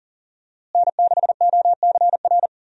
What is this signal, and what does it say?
Morse code, heard as a keyed tone, about 37 wpm, 700 Hz. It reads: N6OCR
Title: USA Call Signs - 5 Characters - Repeated 2x - 40wpm